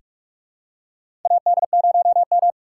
Morse code, heard as a keyed tone, about 43 wpm, 700 Hz. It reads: AD0M
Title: USA Call Signs - 3x Character Spacing - 45wpm